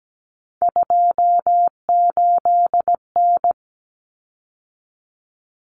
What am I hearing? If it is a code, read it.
Morse code, heard as a keyed tone, about 17 wpm, 700 Hz. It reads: 28N
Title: Common Cut Numbers - 3-Digit 17wpm